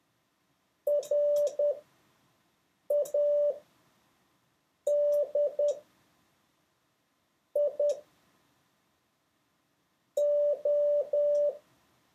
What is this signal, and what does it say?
Morse code, heard as a keyed tone, about 10 wpm, spaced as for 4 wpm, 580 Hz. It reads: RADIO
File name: 6of6-test